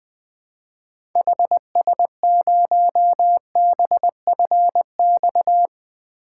Morse code, heard as a keyed tone, about 20 wpm, 700 Hz. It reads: HS0BFX
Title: 2023 POTA Activator Call Signs - 20wpm